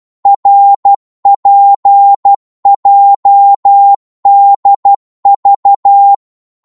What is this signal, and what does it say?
Morse code, heard as a keyed tone, about 12 wpm, 800 Hz. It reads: RPJDV
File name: cwmp3 / ltr-12wpm